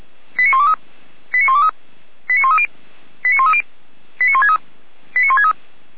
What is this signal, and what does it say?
Five-tone selective call (ZVEI), heard as quick runs of five tones, tones of about 70 ms. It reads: 89123 89123 89130 89130 89163 89163